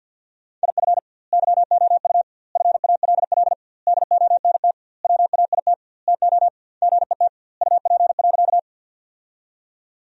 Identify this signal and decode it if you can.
Morse code, heard as a keyed tone, about 46 wpm, 700 Hz. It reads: IF YOU FALL DONT WAIT TO GET UP.
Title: 2x Word-Spacing - Sentences from Top 500 Words - 50wpm